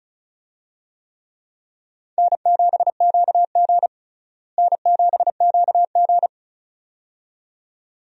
Morse code, heard as a keyed tone, about 36 wpm, 700 Hz. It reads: N7QG N7QG
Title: USA Call Signs - 4 Characters - Repeated 2x - 35wpm